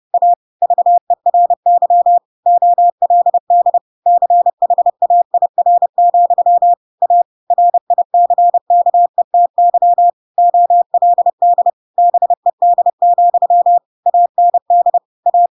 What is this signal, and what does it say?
Morse code, heard as a keyed tone, about 30 wpm, 700 Hz. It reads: A VERY OLD CHAIR, A RICKETY OLD BED, AND A
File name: Pinocchio / Pinocchio ch3 30 30 ik1hge